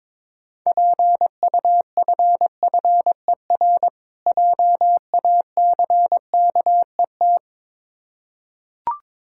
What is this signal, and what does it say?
Morse code, heard as a keyed tone, about 22 wpm, 690 Hz. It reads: PUFFER JACKET
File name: Difficult Words with Hints - 22wpm